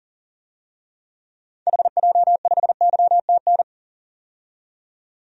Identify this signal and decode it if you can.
Morse code, heard as a keyed tone, about 40 wpm, 700 Hz. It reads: HJ5YTN
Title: DXCC-100 Call Signs - 6 Characters - 40wpm